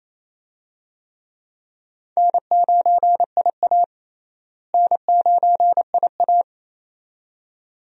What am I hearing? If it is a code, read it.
Morse code, heard as a keyed tone, about 28 wpm, 700 Hz. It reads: N9IA N9IA
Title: USA Call Signs - 4 Characters - Repeated 2x - 28wpm